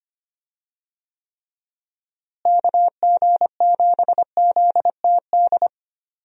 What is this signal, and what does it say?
Morse code, heard as a keyed tone, about 25 wpm, 700 Hz. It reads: KG7ZTD
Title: USA Call Signs - 6 Characters - Repeated 3x - 25wpm